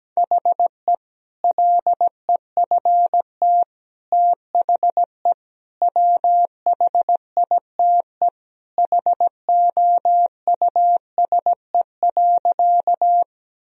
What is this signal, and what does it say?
Morse code, heard as a keyed tone, about 17 wpm, 700 Hz. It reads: HE LEFT THE WHITE HOUSE.
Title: Sentences from Top 1000 Words - Review Part 1 - 17wpm